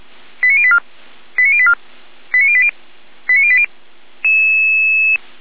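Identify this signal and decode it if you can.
Five-tone selective call (ZVEI), heard as quick runs of five tones, tones of about 70 ms. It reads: 89084 89084 89080 89080
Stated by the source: ZVEI five-tone selective call received with an SDR, recorded on the Feuerwehr Orlamünde audio server.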